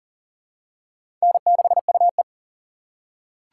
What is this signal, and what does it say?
Morse code, heard as a keyed tone, about 40 wpm, 700 Hz. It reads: N6UE
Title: USA Call Signs - 4 Characters - 4x Character Spacing - 40wpm